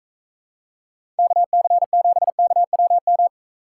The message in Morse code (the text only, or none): KC7KWM